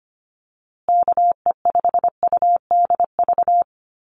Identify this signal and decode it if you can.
Morse code, heard as a keyed tone, about 25 wpm, 700 Hz. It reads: KE5UDV
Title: USA Call Signs Repeated 2x 25wpm